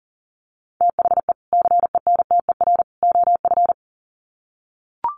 Morse code, heard as a keyed tone, about 38 wpm, 700 Hz. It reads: THE CENTER OF
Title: Sets of 3 Words 40wpm